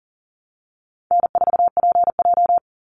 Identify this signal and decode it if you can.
Morse code, heard as a keyed tone, about 40 wpm, 700 Hz. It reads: N4PJ
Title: Easy Call Signs Repeated 3x - 40wpm